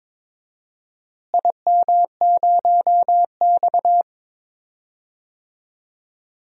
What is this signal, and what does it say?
Morse code, heard as a keyed tone, about 22 wpm, 700 Hz. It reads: IM0X
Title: DXCC-100 Call Signs - 4 Characters - Repeated 2x - 22wpm